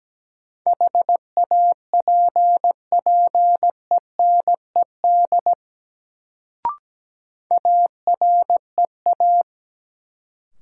Morse code, heard as a keyed tone, about 17 wpm, 700 Hz. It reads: HAPPENED AREA
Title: Sentences from Top 1000 Words - Review Part 2 - New Word Drill - 17wpm